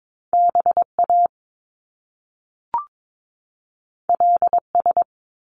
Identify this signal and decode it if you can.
Morse code, heard as a keyed tone, about 22 wpm, 700 Hz. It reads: B A L S
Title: Top Two Letter Combinations in Words 22wpm